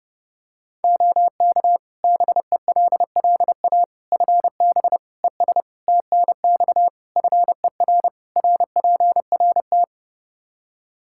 Morse code, heard as a keyed tone, about 30 wpm, 700 Hz. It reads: OK BELLA FB ES TNX FER RPRT
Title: K1USN SST QSO Practice 30wpm